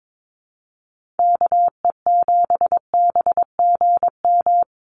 Morse code, heard as a keyed tone, about 22 wpm, 700 Hz. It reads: KE7BGM
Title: NA SOTA Activator Callsigns 22wpm